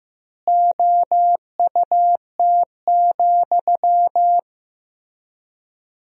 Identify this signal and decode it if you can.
Morse code, heard as a keyed tone, about 16 wpm, 700 Hz. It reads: OUT,
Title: 4x Word-Spacing - Sentences from Top 300 Words - 15wpm